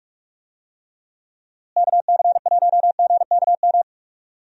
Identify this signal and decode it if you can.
Morse code, heard as a keyed tone, about 41 wpm, 700 Hz. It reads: KC1GKM
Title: USA Call Signs - 6 Characters - Repeated 2x - 45wpm